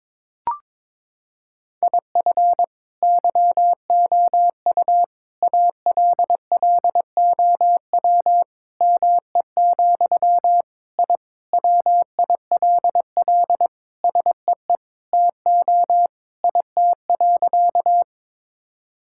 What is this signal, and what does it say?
Morse code, heard as a keyed tone, about 22 wpm, 700 Hz. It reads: IF YOU ALLOW ME, I WILL SEE TO IT.
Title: Sentences from Top 1000 Words - Review Part 5 - 22wpm